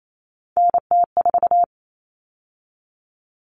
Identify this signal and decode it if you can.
Morse code, heard as a keyed tone, about 28 wpm, 700 Hz. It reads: NT4
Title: All Cut Numbers - 3-Digit 28wpm